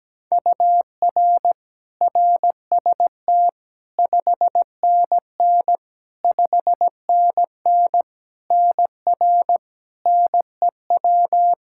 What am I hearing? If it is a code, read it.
Morse code, heard as a keyed tone, about 17 wpm, 700 Hz. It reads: UR RST 5NN 5NN NR NEW
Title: Practice QSO - Exchange 1 - Beginning - 17wpm